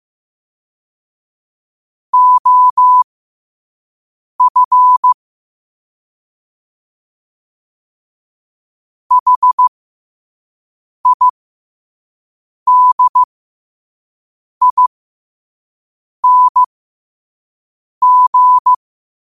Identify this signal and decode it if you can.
Morse code, heard as a keyed tone, about 15 wpm, spaced as for 5 wpm, 1000 Hz. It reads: OF HIDING
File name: qotdfiles / 5wpm